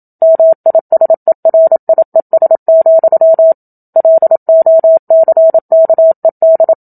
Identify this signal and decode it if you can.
Morse code, heard as a keyed tone, about 27 wpm, 650 Hz. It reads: MISERIES, LOCKED
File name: Book Two / 01-Chapter 1 - 27 WPM